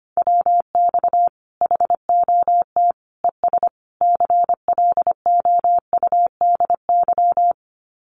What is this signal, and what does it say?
Morse code, heard as a keyed tone, about 25 wpm, 700 Hz. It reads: WX HOT ES CLOUDY
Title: Common QSO Phrases 25wpm